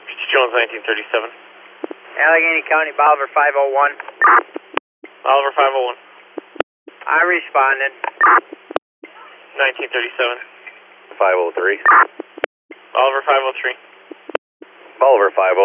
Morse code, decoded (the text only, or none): none